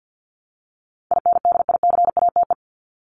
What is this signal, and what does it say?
Morse code, heard as a keyed tone, about 49 wpm, 700 Hz. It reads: INDICATE
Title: Top 1251-1500 Words 50wpm